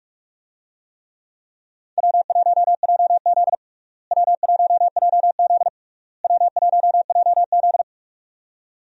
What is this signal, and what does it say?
Morse code, heard as a keyed tone, about 46 wpm, 700 Hz. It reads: W1JZ W1JZ W1JZ
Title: USA Call Signs - 4 Characters - Repeated 3x - 45wpm